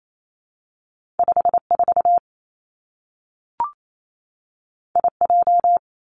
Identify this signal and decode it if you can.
Morse code, heard as a keyed tone, about 28 wpm, 700 Hz. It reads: 5 4 I J
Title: Two Challenging Letter-Number Combinations 28wpm